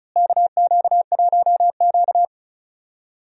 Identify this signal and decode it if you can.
Morse code, heard as a keyed tone, about 34 wpm, 700 Hz. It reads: KQ1Q
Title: USA Call Signs - 4 Characters - Repeated 3x - 35wpm